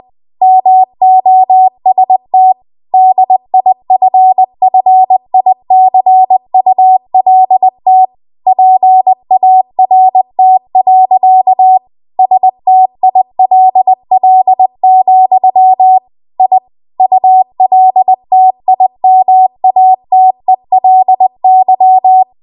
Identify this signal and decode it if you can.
Morse code, heard as a keyed tone, about 20 wpm, 750 Hz. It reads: MOST DIFFICULT PART. STILL, I ULTIMATELY
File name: Archive / 20 WPM / 20WPM